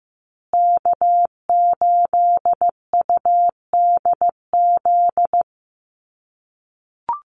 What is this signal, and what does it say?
Morse code, heard as a keyed tone, about 15 wpm, 700 Hz. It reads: K8UDZ
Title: USA Call Signs Repeated 3x 15wpm